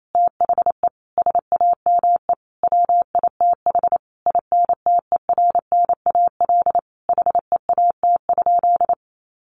Morse code, heard as a keyed tone, about 28 wpm, 700 Hz. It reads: THE SAME WITH INTERNAL HEAT?
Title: Sentences from Top 1400 Words 28wpm